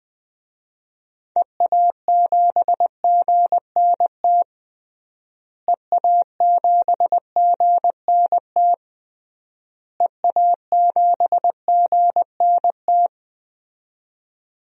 Morse code, heard as a keyed tone, about 21 wpm, 700 Hz. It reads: EA7GNT EA7GNT EA7GNT
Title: DXCC-100 Call Signs - 6 Characters - Repeated 3x - 20wpm